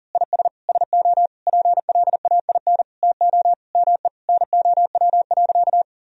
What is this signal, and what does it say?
Morse code, heard as a keyed tone, about 40 wpm, 700 Hz. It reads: IS SO PLAIN TO ME NOW.
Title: Sentences from Top 1000 Words - Review Part 2 - 40wpm